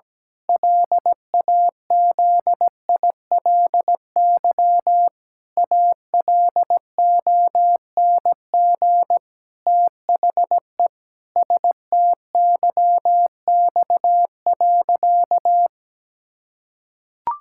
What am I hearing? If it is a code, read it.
Morse code, heard as a keyed tone, about 17 wpm, 700 Hz. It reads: LAZILY ALONG THE STYX.
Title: Pangram Challenge v2 - 17wpm